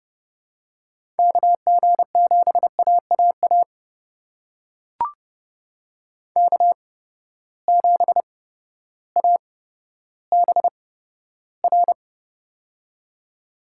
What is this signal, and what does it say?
Morse code, heard as a keyed tone, about 30 wpm, spaced as for 14 wpm, 700 Hz. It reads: KG7AAA K7ABR
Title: USA Call Signs - 8x Character Spacing - 30wpm